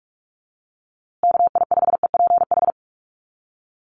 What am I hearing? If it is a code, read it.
Morse code, heard as a keyed tone, about 44 wpm, 700 Hz. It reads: KI5EPH
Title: USA Call Signs Repeated 2x 45wpm